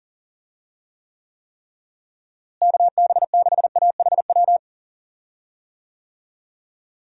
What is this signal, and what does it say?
Morse code, heard as a keyed tone, about 40 wpm, 700 Hz. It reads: KB6AHW
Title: USA Call Signs - 6 Characters - Repeated 3x - 40wpm